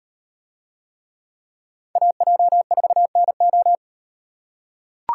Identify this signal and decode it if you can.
Morse code, heard as a keyed tone, about 38 wpm, 700 Hz. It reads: AJ4NO